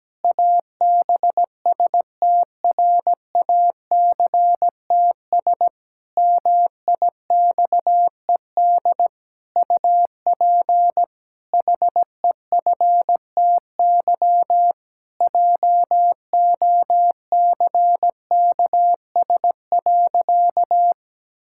Morse code, heard as a keyed tone, about 17 wpm, 700 Hz. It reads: ABSTRACTS MIXED UP HEFTY JOCKS.
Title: Pangram Challenge v2 - 17wpm